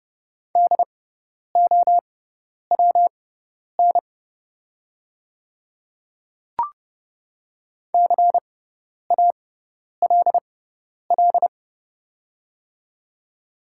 Morse code, heard as a keyed tone, about 30 wpm, spaced as for 10 wpm, 700 Hz. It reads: DOWN CALL
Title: 6x Character Spacing - Top 300 Words - 30wpm